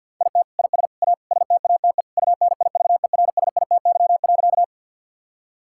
Morse code, heard as a keyed tone, about 50 wpm, 700 Hz. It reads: IT IS A STATE UNIVERSITY.
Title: Sentences from Top 1000 Words - Review Part 5 - 50wpm